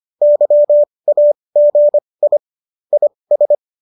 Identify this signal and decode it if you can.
Morse code, heard as a keyed tone, about 25 wpm, spaced as for 20 wpm, 590 Hz. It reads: YAGI IS